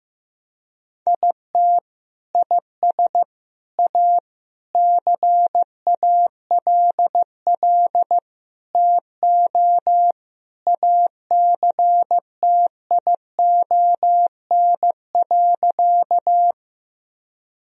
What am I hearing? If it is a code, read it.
Morse code, heard as a keyed tone, about 16 wpm, 700 Hz. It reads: IT IS A CALL TO ACTION.
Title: Sentences from Top 1000 Words - Review Part 4 - 15wpm